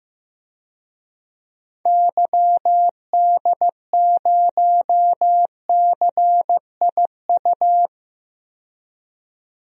Morse code, heard as a keyed tone, about 15 wpm, 700 Hz. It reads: YD0CIU
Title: DXCC-100 Call Signs - 6 Characters - 15wpm